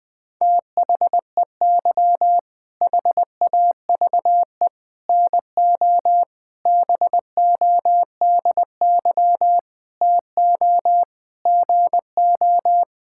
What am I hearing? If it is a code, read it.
Morse code, heard as a keyed tone, about 20 wpm, 700 Hz. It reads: THEY HAVE NO BODY TO GO
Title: All-Age Appropriate Jokes - 20wpm